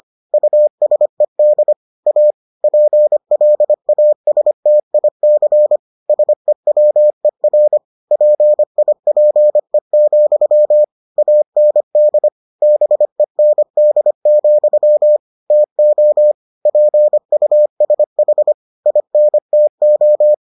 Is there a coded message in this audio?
Yes: Morse code from 0.3 to 20.3 s, heard as a keyed tone, about 25 wpm, 600 Hz. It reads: USED A PLASTIC SEWER PIPE, AND BEND, TO PUSH INTO